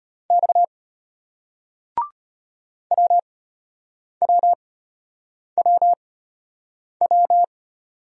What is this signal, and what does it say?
Morse code, heard as a keyed tone, about 32 wpm, spaced as for 8 wpm, 700 Hz. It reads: X WWWW